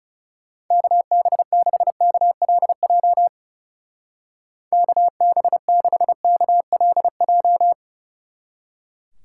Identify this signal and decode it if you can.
Morse code, heard as a keyed tone, about 37 wpm, 700 Hz. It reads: KB6KLJ KB6KLJ
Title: Speed Racing - USA 2x3 Call Signs 30wpm